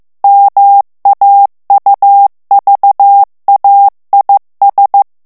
Morse code, heard as a keyed tone, about 15 wpm, 800 Hz. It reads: MAUVAIS
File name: morse mp3 / FRANCAIS TEXTE CLAIR 1 VITESSE 14 MOTS MINUTES